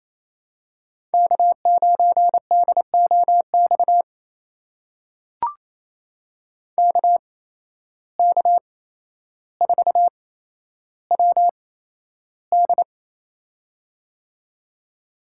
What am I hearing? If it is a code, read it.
Morse code, heard as a keyed tone, about 28 wpm, spaced as for 12 wpm, 700 Hz. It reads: K9DOX KK4WD
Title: USA Call Signs - 5 Characters - 8x Character Spacing - 28wpm